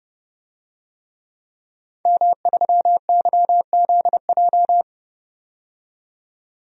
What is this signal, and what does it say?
Morse code, heard as a keyed tone, about 31 wpm, 700 Hz. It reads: M3YZJ
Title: DXCC-100 Call Signs - 5 Characters - 30wpm